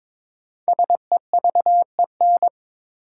Morse code, heard as a keyed tone, about 23 wpm, 700 Hz. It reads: SEVEN
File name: 5 Letter Words - First 3 Letters Given - 22wpm